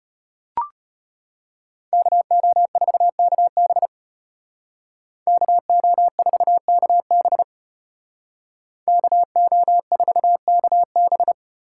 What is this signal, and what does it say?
Morse code, heard as a keyed tone, about 33 wpm, 700 Hz. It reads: KO4KB KO4KB KO4KB